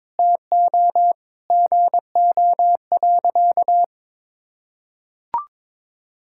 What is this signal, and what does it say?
Morse code, heard as a keyed tone, about 22 wpm, 700 Hz. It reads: TO GO.
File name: practice / 22wpm / Sentences with 4-Word Phrases - 22wpm